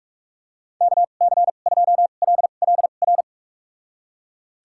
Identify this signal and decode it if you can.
Morse code, heard as a keyed tone, about 45 wpm, spaced as for 32 wpm, 700 Hz. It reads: KC2LLR
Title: USA Call Signs - 6 Characters - 2x Character Spacing - 45wpm